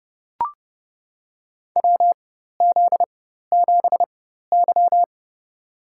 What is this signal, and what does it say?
Morse code, heard as a keyed tone, about 32 wpm, spaced as for 14 wpm, 700 Hz. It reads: WZ7Y
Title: USA Call Signs - 4 Characters - 4x Character Spacing - 30wpm